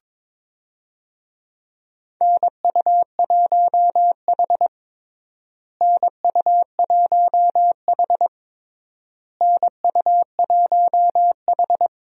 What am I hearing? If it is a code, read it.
Morse code, heard as a keyed tone, about 22 wpm, 700 Hz. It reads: NU1H NU1H NU1H
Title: USA Call Signs - 4 Characters - Repeated 3x - 22wpm